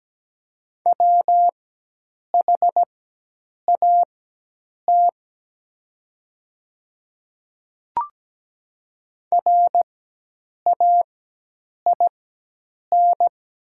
Morse code, heard as a keyed tone, about 17 wpm, spaced as for 8 wpm, 700 Hz. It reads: WHAT RAIN
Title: 4x Character Spacing - Top 300 Words - 17wpm